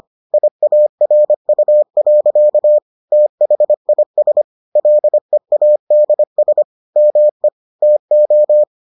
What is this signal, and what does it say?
Morse code, heard as a keyed tone, about 25 wpm, 600 Hz. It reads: IARU. THIS LEADS ME TO